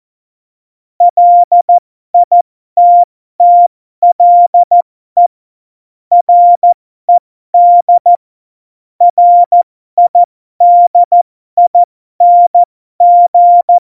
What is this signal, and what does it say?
Morse code, heard as a keyed tone, about 14 wpm, spaced as for 12 wpm, 700 Hz. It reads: LITTLE RED RIDING
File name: Morsekurs / Witze